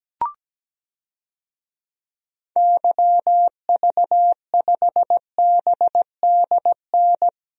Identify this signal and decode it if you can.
Morse code, heard as a keyed tone, about 17 wpm, 700 Hz. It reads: YV5BDN
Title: DXCC-100 Call Signs - 6 Characters - Repeated 2x - 17wpm